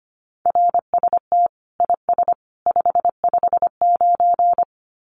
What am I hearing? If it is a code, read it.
Morse code, heard as a keyed tone, about 25 wpm, 700 Hz. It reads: RST IS 559
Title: Common QSO Phrases 25wpm